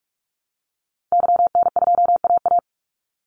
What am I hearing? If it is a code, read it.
Morse code, heard as a keyed tone, about 44 wpm, 700 Hz. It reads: YN2AA
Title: Call Signs Repeated 3x - 45wpm